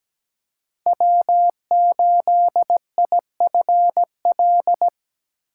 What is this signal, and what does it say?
Morse code, heard as a keyed tone, about 20 wpm, 700 Hz. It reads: W8IFL
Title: Speed Racing - USA 1x3 Call Signs 17wpm